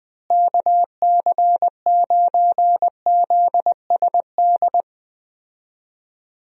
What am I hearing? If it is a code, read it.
Morse code, heard as a keyed tone, about 20 wpm, 700 Hz. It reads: KC9ZSD
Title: USA Call Signs - 6 Characters - Repeated 3x - 20wpm